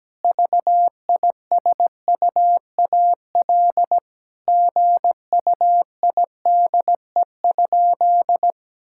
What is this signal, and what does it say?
Morse code, heard as a keyed tone, about 17 wpm, 700 Hz. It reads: VISUAL GUIDE?
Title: Sentences from Top 1000 Words - Review Part 6 - 17wpm